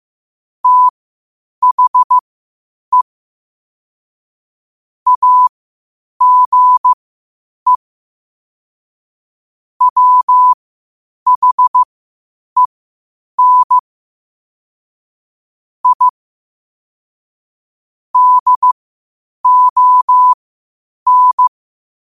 Morse code, heard as a keyed tone, about 15 wpm, spaced as for 8 wpm, 1000 Hz. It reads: THE AGE WHEN I DON
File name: qotdfiles / 8wpm